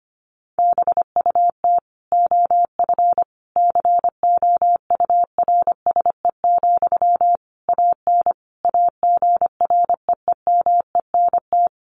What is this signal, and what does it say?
Morse code, heard as a keyed tone, about 25 wpm, 700 Hz. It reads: BUT OF COURSE, AN AGREEMENT
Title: Sentences from Top 1500 Words 25wpm